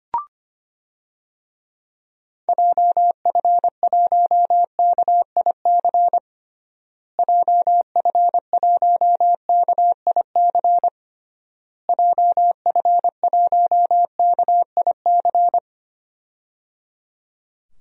Morse code, heard as a keyed tone, about 25 wpm, 700 Hz. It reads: JF1KIC JF1KIC JF1KIC
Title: DXCC-100 Call Signs - 6 Characters - Repeated 3x - 25wpm